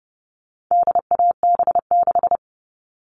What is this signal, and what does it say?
Morse code, heard as a keyed tone, about 30 wpm, 700 Hz. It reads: DAB6
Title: All Cut Numbers - 4-Digit 30wpm